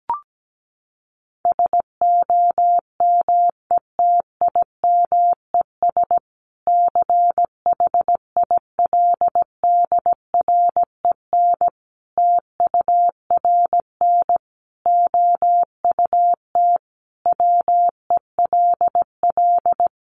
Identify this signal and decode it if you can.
Morse code, heard as a keyed tone, about 17 wpm, 700 Hz. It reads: SOMETIMES CHILDREN TURN OUT WELL
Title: Sentences from Top 600 Words 17wpm